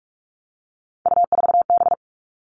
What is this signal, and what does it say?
Morse code, heard as a keyed tone, about 46 wpm, 700 Hz. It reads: U4B
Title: Three Challenging Letter-Number Combinations 45wpm